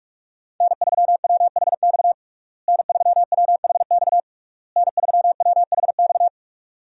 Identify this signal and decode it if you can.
Morse code, heard as a keyed tone, about 47 wpm, 700 Hz. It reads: N3WHX N3WHX N3WHX
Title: USA Call Signs - 5 Characters - Repeated 3x - 45wpm